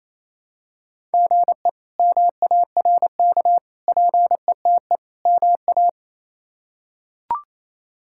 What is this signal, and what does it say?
Morse code, heard as a keyed tone, about 28 wpm, 700 Hz. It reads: GE MARK PETE MA
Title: K1USN SST QSO Practice - 28wpm